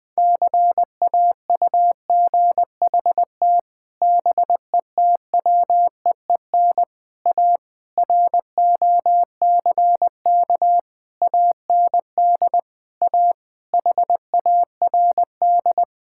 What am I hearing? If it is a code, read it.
Morse code, heard as a keyed tone, about 20 wpm, 700 Hz. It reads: CAUGHT BETWEEN A ROCK AND A HARD
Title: Sets of 8 Words - 20wpm